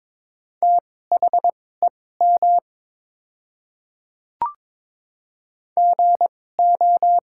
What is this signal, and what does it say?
Morse code, heard as a keyed tone, about 22 wpm, spaced as for 16 wpm, 700 Hz. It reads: THEM GO
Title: 2x Character Spacing - Top 100 Words - 22wpm